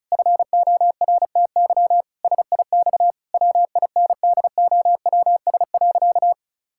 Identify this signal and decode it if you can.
Morse code, heard as a keyed tone, about 34 wpm, 700 Hz. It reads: FORTY SIX WINDOWS.
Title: Pangram Challenge v2 - 35wpm